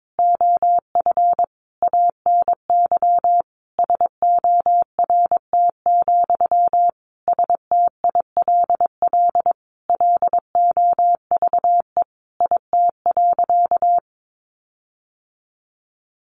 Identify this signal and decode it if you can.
Morse code, heard as a keyed tone, about 22 wpm, 700 Hz. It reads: OF ANY SORT, STILL LOVE IT.
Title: Sentences from Top 1500 Words 22wpm